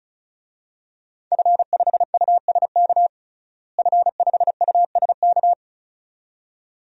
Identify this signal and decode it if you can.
Morse code, heard as a keyed tone, about 36 wpm, 700 Hz. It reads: F5USK F5USK
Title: DXCC-100 Call Signs - 5 Characters - Repeated 2x - 35wpm